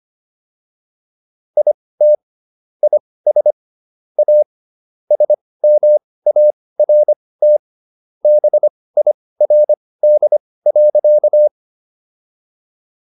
Morse code, heard as a keyed tone, about 25 wpm, spaced as for 18 wpm, 600 Hz. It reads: IT IS A SMART BIRD.